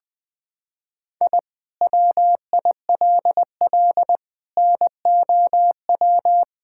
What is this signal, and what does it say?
Morse code, heard as a keyed tone, about 20 wpm, 700 Hz. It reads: I WILL NOW
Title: Sentences from Top 1000 Words - Review Part 6 - 20wpm